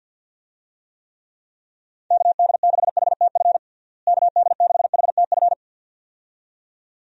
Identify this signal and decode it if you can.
Morse code, heard as a keyed tone, about 52 wpm, 700 Hz. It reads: KD6HTF KD6HTF
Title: USA Call Signs - 6 Characters - Repeated 2x - 50wpm